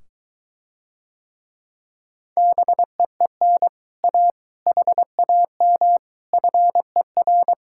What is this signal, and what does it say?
Morse code, heard as a keyed tone, about 23 wpm, 700 Hz. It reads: BEEN A HAM FER